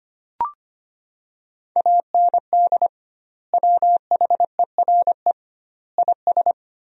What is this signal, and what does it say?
Morse code, heard as a keyed tone, about 25 wpm, 700 Hz. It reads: AND WHERE IS
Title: 2x Word-Spacing - Sentences from Top 200 Words - 25wpm